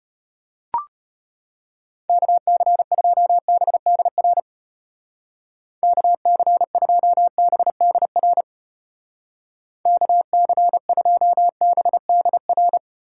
Morse code, heard as a keyed tone, about 31 wpm, 700 Hz. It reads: KC2BDR KC2BDR KC2BDR
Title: Speed Racing - USA 2x3 Call Signs 25wpm